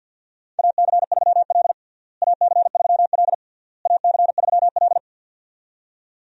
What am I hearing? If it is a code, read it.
Morse code, heard as a keyed tone, about 46 wpm, 700 Hz. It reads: AC3L AC3L AC3L
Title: USA Call Signs - 4 Characters - Repeated 3x - 50wpm